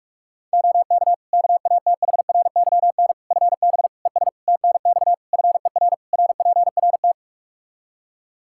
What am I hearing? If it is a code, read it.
Morse code, heard as a keyed tone, about 46 wpm, 700 Hz. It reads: OK KATHRYN FB ES TNX FER RPRT
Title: K1USN SST QSO Practice 45wpm